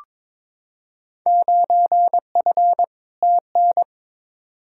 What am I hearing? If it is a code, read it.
Morse code, heard as a keyed tone, about 21 wpm, 700 Hz. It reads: 9F TN
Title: Field Day Exchanges Encore - 22wpm